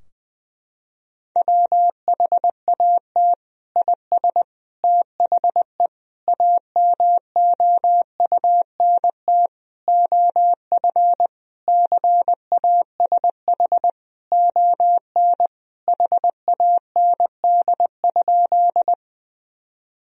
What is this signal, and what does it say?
Morse code, heard as a keyed tone, about 20 wpm, 700 Hz. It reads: WHAT IS THE AMOUNT OF CASH ON HAND?
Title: Sentences from Top 1000 Words - Review Part 4 - 20wpm